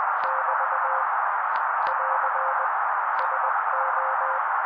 Morse code, heard as a keyed tone, about 20 wpm, 530 Hz. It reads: <BT> <AR> SO
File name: daily / 18wpm noise